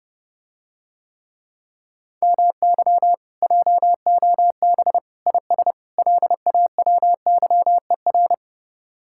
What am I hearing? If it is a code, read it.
Morse code, heard as a keyed tone, about 30 wpm, 700 Hz. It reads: MY JOB IS LAWYER